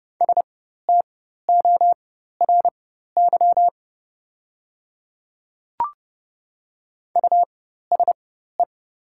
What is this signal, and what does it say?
Morse code, heard as a keyed tone, about 30 wpm, spaced as for 14 wpm, 700 Hz. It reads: STORY USE
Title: 4x Character Spacing - Top 300 Words - 30wpm